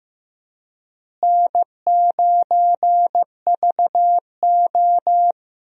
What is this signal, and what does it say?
Morse code, heard as a keyed tone, about 15 wpm, 700 Hz. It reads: N9VO
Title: USA Call Signs - 4 Characters - Repeated 2x - 15wpm